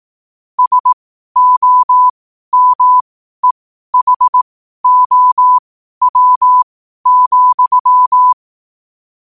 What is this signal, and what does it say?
Morse code, heard as a keyed tone, about 18 wpm, spaced as for 12 wpm, 1000 Hz. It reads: SOMEHOW,
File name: qotdfiles / 12wpm